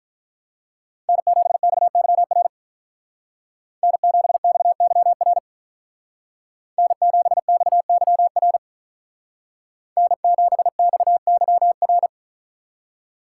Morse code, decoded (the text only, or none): N7XYR N7XYR N7XYR N7XYR